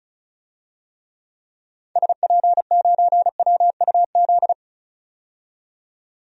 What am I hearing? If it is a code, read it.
Morse code, heard as a keyed tone, about 36 wpm, 700 Hz. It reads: SP9WUZ
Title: DXCC-100 Call Signs - 6 Characters - 35wpm